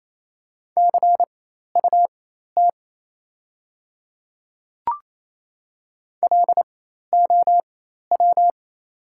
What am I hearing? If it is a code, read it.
Morse code, heard as a keyed tone, about 28 wpm, spaced as for 13 wpm, 690 Hz. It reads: CUT LOW
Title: 4x Character Spacing - Top 300 Words - 28wpm